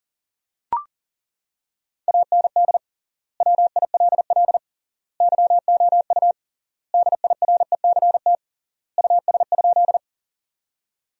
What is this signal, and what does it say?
Morse code, heard as a keyed tone, about 40 wpm, 700 Hz. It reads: AND WILL YOU DIRECT US?
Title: 3x Word-Spacing - Sentences from Top 400 Words - 40wpm